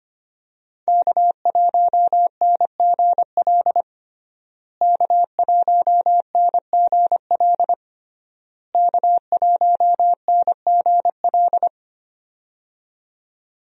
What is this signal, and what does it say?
Morse code, heard as a keyed tone, about 25 wpm, 700 Hz. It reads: K1NGL K1NGL K1NGL
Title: USA Call Signs - 5 Characters - Repeated 3x - 25wpm